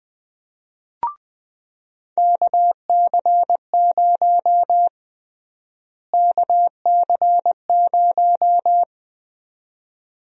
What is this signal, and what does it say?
Morse code, heard as a keyed tone, about 20 wpm, 700 Hz. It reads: KC0 KC0
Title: USA Call Sign Prefixes - Repeated 2x - 20wpm